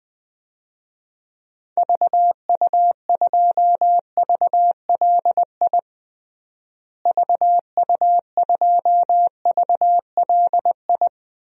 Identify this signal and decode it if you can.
Morse code, heard as a keyed tone, about 21 wpm, 700 Hz. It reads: VU2VLI VU2VLI
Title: DXCC-100 Call Signs - 6 Characters - Repeated 2x - 20wpm